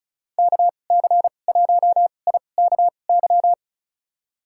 Morse code, heard as a keyed tone, about 34 wpm, spaced as for 25 wpm, 700 Hz. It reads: KC1IKY
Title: USA Call Signs - 2x Character Spacing - 35wpm